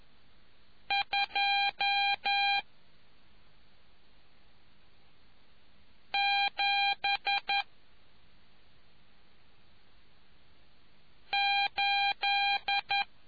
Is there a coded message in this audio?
Yes: Morse code from 0.9 to 13.0 s, heard as a keyed tone, about 11 wpm, 800 Hz. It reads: 2 7 8